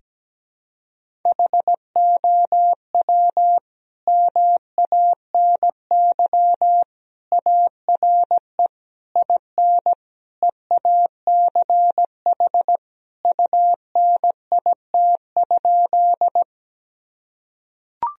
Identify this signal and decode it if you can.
Morse code, heard as a keyed tone, about 17 wpm, 700 Hz. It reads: HOW MANY ARE IN EACH UNIT?
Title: Sentences from Top 1000 Words - Review Part 2 - 17wpm